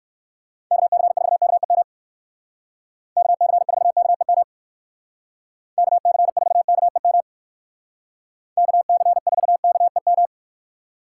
Morse code, heard as a keyed tone, about 60 wpm, 700 Hz. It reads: KC4KEM KC4KEM KC4KEM KC4KEM